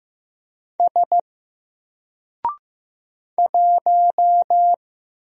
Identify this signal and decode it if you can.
Morse code, heard as a keyed tone, about 15 wpm, 700 Hz. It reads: S 1
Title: Single Character (Letter:Number:Punctuation:Prosign) Alt 15wpm